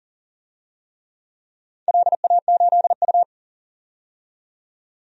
Morse code, heard as a keyed tone, about 37 wpm, 700 Hz. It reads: LA8U
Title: DXCC-100 Call Signs - 4 Characters - 40wpm